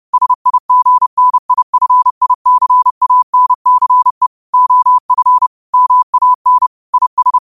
Morse code, heard as a keyed tone, about 30 wpm, 1000 Hz. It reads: SIGNIFICANCE OF MAN IS